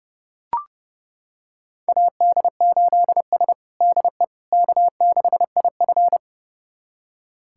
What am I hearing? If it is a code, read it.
Morse code, heard as a keyed tone, about 32 wpm, 700 Hz. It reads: AD8S DE K6IF
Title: USA Callsign DE Callsign - 4 Characters - 30wpm